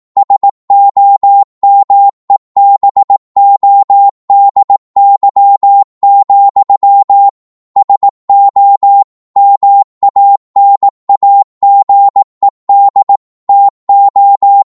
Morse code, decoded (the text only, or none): SOMEBODY, SO MANAGED TO